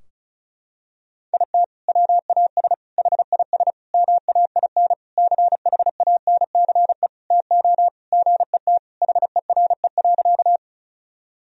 Morse code, heard as a keyed tone, about 34 wpm, 700 Hz. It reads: IT WAS HIS MAIN CHANCE TO GET HERE.
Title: Sentences from Top 1000 Words - Review Part 2 - 35wpm